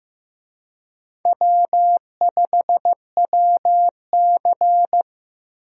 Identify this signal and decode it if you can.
Morse code, heard as a keyed tone, about 16 wpm, 700 Hz. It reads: W5WC
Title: USA Call Signs - 4 Characters - Repeated 3x - 15wpm